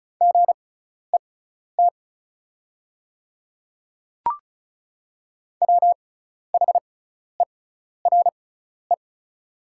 Morse code, heard as a keyed tone, about 34 wpm, spaced as for 12 wpm, 700 Hz. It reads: GET WHERE
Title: 6x Character Spacing - Top 300 Words - 35wpm